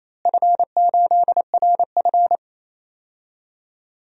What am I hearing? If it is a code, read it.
Morse code, heard as a keyed tone, about 28 wpm, 700 Hz. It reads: F8RF
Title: DXCC-100 Call Signs - 4 Characters - Repeated 3x - 28wpm